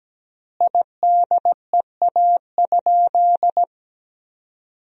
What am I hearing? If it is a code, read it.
Morse code, heard as a keyed tone, about 17 wpm, 700 Hz. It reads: IDEA?
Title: 3x Word-Spacing - Sentences from Top 400 Words - 17wpm